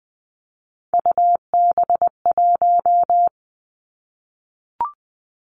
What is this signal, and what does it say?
Morse code, heard as a keyed tone, about 20 wpm, 700 Hz. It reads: UB1
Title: Three Challenging Letter-Number Combinations 20wpm